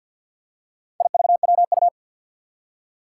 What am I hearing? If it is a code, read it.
Morse code, heard as a keyed tone, about 47 wpm, 700 Hz. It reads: IVWU
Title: Four Challenging Letter-Number Combinations 50wpm